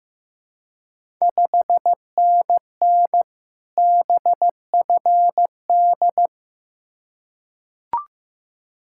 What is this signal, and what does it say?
Morse code, heard as a keyed tone, about 15 wpm, 700 Hz. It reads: 5NN BFD
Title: Ontario QSO Party Exchange 15wpm